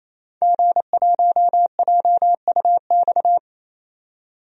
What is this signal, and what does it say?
Morse code, heard as a keyed tone, about 28 wpm, 700 Hz. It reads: G1JUX
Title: DXCC-100 Call Signs - 5 Characters - 28wpm